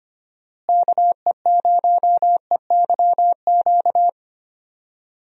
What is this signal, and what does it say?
Morse code, heard as a keyed tone, about 25 wpm, 700 Hz. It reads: KE0EYQ
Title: USA Call Signs - 6 Characters - Repeated 2x - 25wpm